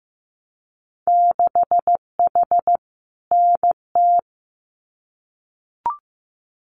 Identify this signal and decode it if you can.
Morse code, heard as a keyed tone, about 15 wpm, 700 Hz. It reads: 6H NT
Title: Winter Field Day Exchanges Encore 15wpm